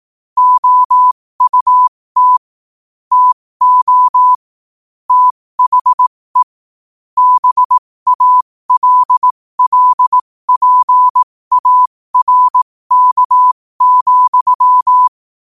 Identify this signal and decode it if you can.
Morse code, heard as a keyed tone, about 18 wpm, spaced as for 15 wpm, 1000 Hz. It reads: OUT TO THE BALLPARK,
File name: qotdfiles / 15wpm